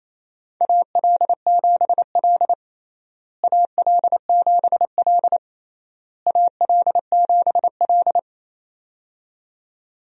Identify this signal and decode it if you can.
Morse code, heard as a keyed tone, about 28 wpm, 700 Hz. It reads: AL7L AL7L AL7L
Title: USA Call Signs - 4 Characters - Repeated 3x - 28wpm